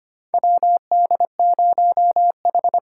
Morse code, heard as a keyed tone, about 25 wpm, 700 Hz. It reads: WD0H
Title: USA Call Signs - 4 Characters - 4x Character Spacing - 25wpm